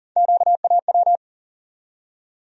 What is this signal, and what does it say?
Morse code, heard as a keyed tone, about 40 wpm, 700 Hz. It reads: QAW
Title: USA Call Sign Postfixes - 3 Letters 40wpm